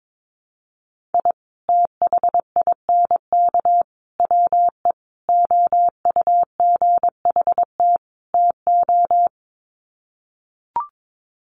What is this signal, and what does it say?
Morse code, heard as a keyed tone, about 22 wpm, 700 Hz. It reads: I THINK WE OUGHT TO
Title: Sets of 5 Words Encore 22wpm